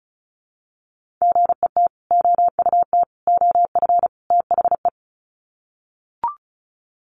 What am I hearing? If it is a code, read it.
Morse code, heard as a keyed tone, about 34 wpm, 700 Hz. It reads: GET OUT OF THE
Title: Sets of 4 Words 35wpm